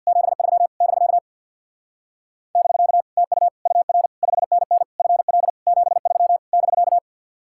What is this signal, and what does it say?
Morse code, heard as a keyed tone, about 60 wpm, 700 Hz. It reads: 73 <BK> <BK> TU UR 5NN FL 73 <BK>
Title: Speed-Racing Common QSO Phrases - 50wpm